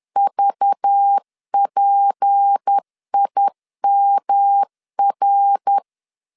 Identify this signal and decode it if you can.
Morse code, heard as a keyed tone, about 10 wpm, 800 Hz. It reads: VPIMR
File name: LEZIONI / 6 PKX0 / 6 P 10wpm 2432